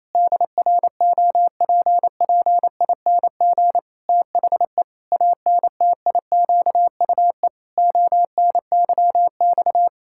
Morse code, heard as a keyed tone, about 28 wpm, 700 Hz. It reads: DROPPING THE ANTIQUE ONYX
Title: Pangram Challenge v2 - 28wpm